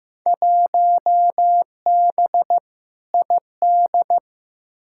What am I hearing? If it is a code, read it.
Morse code, heard as a keyed tone, about 15 wpm, 700 Hz. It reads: 1B ID
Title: Field Day Exchanges - 15wpm